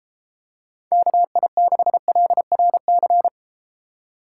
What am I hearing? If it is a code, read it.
Morse code, heard as a keyed tone, about 31 wpm, 700 Hz. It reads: KI6LRC